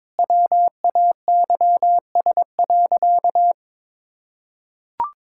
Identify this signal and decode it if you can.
Morse code, heard as a keyed tone, about 22 wpm, 700 Hz. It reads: WAYS.
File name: practice / 22wpm / Sentences with 2-Word Phrases - 22wpm